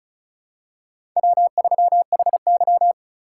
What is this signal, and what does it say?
Morse code, heard as a keyed tone, about 34 wpm, 700 Hz. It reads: W3HY
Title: USA Call Signs - 3x Character Spacing - 35wpm